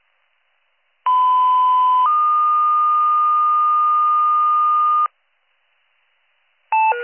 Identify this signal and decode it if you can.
Two-tone page, heard as a one-second tone, then a three-second tone, about 1010 then 1220 Hz. It is heard from 1.1 to 5.1 s.